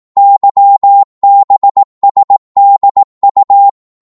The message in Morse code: YBSDU